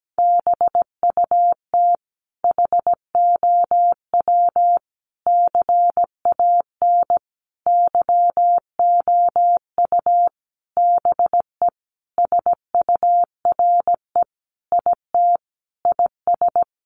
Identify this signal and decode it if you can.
Morse code, heard as a keyed tone, about 17 wpm, 700 Hz. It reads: BUT HOW CAN YOU BE SURE IT IS
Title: Sentences from Top 1400 Words 17wpm